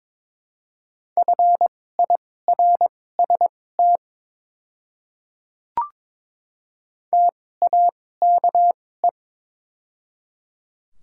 Morse code, heard as a keyed tone, about 22 wpm, spaced as for 16 wpm, 700 Hz. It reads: FIRST TAKE
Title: 2x Character Spacing - Top 300 Words - 22wpm